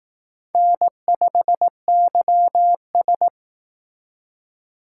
Morse code, heard as a keyed tone, about 18 wpm, 700 Hz. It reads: N5YS
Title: Speed Racing - USA Call Signs 15wpm